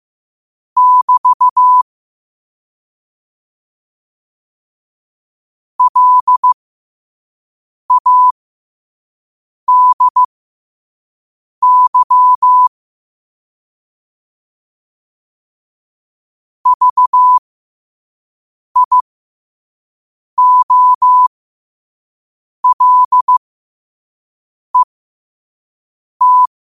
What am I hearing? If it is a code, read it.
Morse code, heard as a keyed tone, about 15 wpm, spaced as for 5 wpm, 1000 Hz. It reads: <BT> LADY VIOLET
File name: qotdfiles / 5wpm